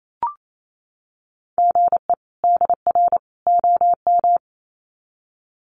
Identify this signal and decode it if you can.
Morse code, heard as a keyed tone, about 28 wpm, 700 Hz. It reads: GE DR OM
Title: Common QSO Phrases 28wpm